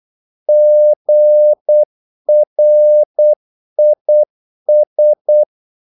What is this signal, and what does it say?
Morse code, heard as a keyed tone, about 8 wpm, 600 Hz. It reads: GRIS
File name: Korte ord DK 8wpm